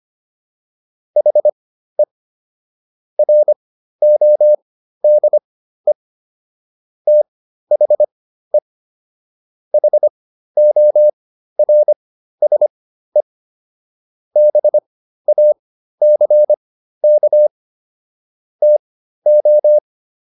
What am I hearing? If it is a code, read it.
Morse code, heard as a keyed tone, about 25 wpm, spaced as for 13 wpm, 600 Hz. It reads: HE RODE THE HORSE BACK TO